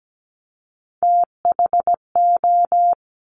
Morse code, heard as a keyed tone, about 17 wpm, 700 Hz. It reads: THO
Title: Top Three Letter Combinations in Words 17wpm